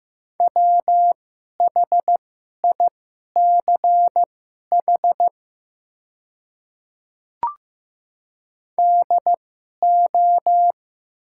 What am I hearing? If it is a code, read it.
Morse code, heard as a keyed tone, about 15 wpm, spaced as for 11 wpm, 700 Hz. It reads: WHICH DO
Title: 2x Character Spacing - Top 100 Words - 15wpm